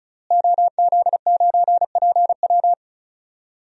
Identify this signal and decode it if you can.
Morse code, heard as a keyed tone, about 36 wpm, 700 Hz. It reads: OZ9PW
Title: DXCC-100 Call Signs - 5 Characters - Repeated 2x - 35wpm